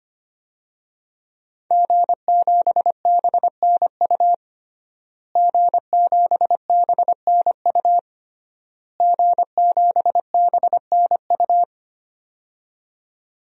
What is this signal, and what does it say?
Morse code, heard as a keyed tone, about 25 wpm, 700 Hz. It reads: G7BNU G7BNU G7BNU
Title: DXCC-100 Call Signs - 5 Characters - Repeated 3x - 25wpm